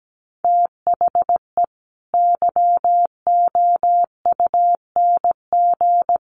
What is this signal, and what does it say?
Morse code, heard as a keyed tone, about 17 wpm, 700 Hz. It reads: THE YOUNG
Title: Sentences from Top 1200 Words 17wpm